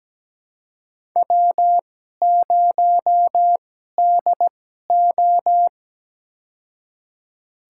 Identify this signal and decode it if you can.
Morse code, heard as a keyed tone, about 17 wpm, spaced as for 12 wpm, 700 Hz. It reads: W0DO
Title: USA Call Signs - 4 Characters - 2x Character Spacing - 17wpm